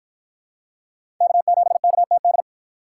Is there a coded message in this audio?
Yes: Morse code from 1.2 to 2.4 s, heard as a keyed tone, about 51 wpm, 700 Hz. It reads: K7KTD